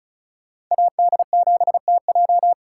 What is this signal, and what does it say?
Morse code, heard as a keyed tone, about 45 wpm, 700 Hz. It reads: AD7TJ